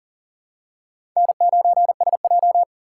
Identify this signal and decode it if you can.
Morse code, heard as a keyed tone, about 39 wpm, 700 Hz. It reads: N9SJ
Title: USA Call Signs - 4 Characters - 4x Character Spacing - 40wpm